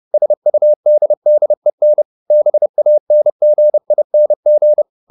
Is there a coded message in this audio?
Yes: Morse code from 0.1 to 4.8 s, heard as a keyed tone, about 30 wpm, 600 Hz. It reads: SUDDEN BANGING